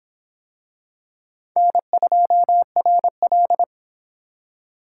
Morse code, heard as a keyed tone, about 24 wpm, 700 Hz. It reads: N2RL